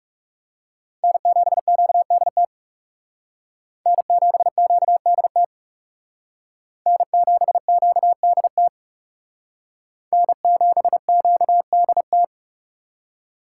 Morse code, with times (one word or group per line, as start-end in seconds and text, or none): 1.0-2.4 N7QDT
3.9-5.4 N7QDT
6.9-8.7 N7QDT
10.1-12.2 N7QDT